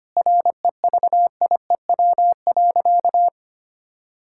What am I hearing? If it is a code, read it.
Morse code, heard as a keyed tone, about 25 wpm, 700 Hz. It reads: REVIEW.
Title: Sentences from Top 1000 Words - Review Part 6 - 25wpm